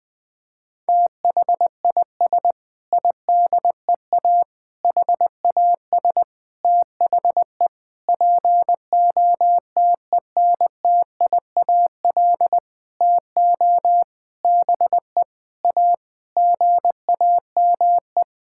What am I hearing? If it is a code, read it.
Morse code, heard as a keyed tone, about 19 wpm, 700 Hz. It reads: THIS IDEA HAS THE POTENTIAL TO BE A GAME
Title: Sentences with 6-Word Phrases - Spoken Beginning - 20wpm